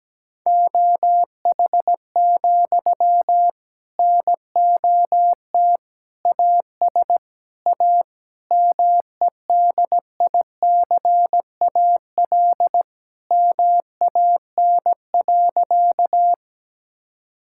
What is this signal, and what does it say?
Morse code, heard as a keyed tone, about 17 wpm, 700 Hz. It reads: OH, NOT AS A MEDICAL MAN.
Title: Sentences from Top 1000 Words - Review Part 5 - 17wpm